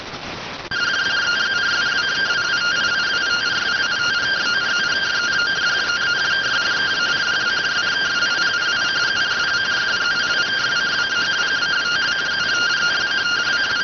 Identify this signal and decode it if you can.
Two-tone teleprinter signal, about 170 Hz shift.